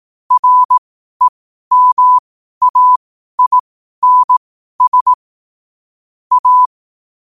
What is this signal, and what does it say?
Morse code, heard as a keyed tone, about 18 wpm, spaced as for 12 wpm, 1000 Hz. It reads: REMAINS A